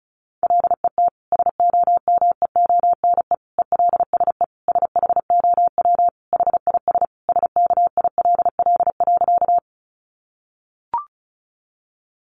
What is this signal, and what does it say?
Morse code, heard as a keyed tone, about 34 wpm, 700 Hz. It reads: LET SOMEONE ELSE SHOW HIS SKILL.
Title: Sentences from Top 1200 Words 35wpm